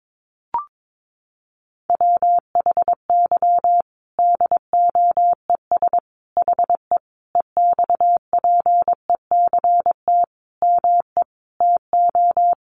Morse code, heard as a keyed tone, about 22 wpm, 700 Hz. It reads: WHY DOES HE EXPECT ME TO
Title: Sentences from Top 1100 Words 22wpm